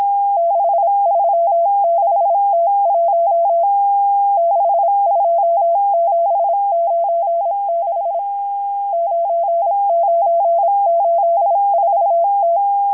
Morse code, heard as a keyed tone, about 26 wpm, 680 Hz. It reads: 636T1 62796 9984T
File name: HM02- -0437utc-7351khz